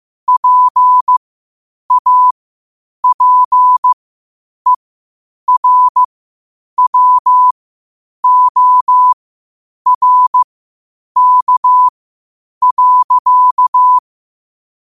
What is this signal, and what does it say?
Morse code, heard as a keyed tone, about 15 wpm, spaced as for 8 wpm, 1000 Hz. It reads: PAPERWORK.
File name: qotdfiles / 8wpm